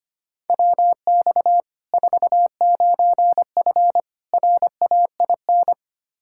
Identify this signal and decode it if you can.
Morse code, heard as a keyed tone, about 25 wpm, 700 Hz. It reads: WX 49F RAIN
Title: Practice QSO - Exchange 2 - Info - 25wpm